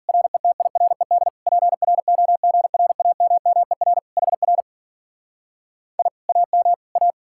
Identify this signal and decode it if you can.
Morse code, heard as a keyed tone, about 50 wpm, 700 Hz. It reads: RETIRED PROGRAMMER HR I AM A